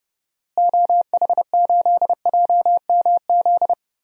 Morse code, heard as a keyed tone, about 30 wpm, 700 Hz. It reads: OH8JMZ
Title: DXCC-100 Call Signs - 6 Characters - Repeated 2x - 30wpm